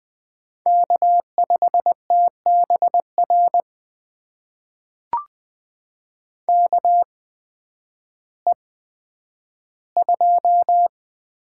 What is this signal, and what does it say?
Morse code, heard as a keyed tone, about 20 wpm, spaced as for 11 wpm, 700 Hz. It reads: K5TBR KE2
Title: USA Call Signs - 5 Characters - 8x Character Spacing - 20wpm